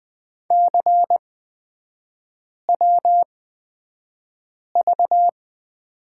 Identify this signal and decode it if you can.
Morse code, heard as a keyed tone, about 20 wpm, spaced as for 5 wpm, 700 Hz. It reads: CWV
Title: 250 Lesson 40e - BK - Three Characters 20wpm